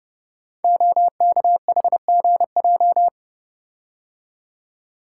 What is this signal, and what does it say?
Morse code, heard as a keyed tone, about 30 wpm, 700 Hz. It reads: OKHGJ
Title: Code Groups - 5 Letter:Numbers - 30wpm